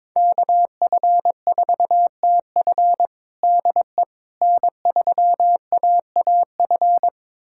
Morse code, heard as a keyed tone, about 22 wpm, 700 Hz. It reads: KF4TF DE N3AAF
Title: USA Callsign DE Callsign - 5 Characters - 22wpm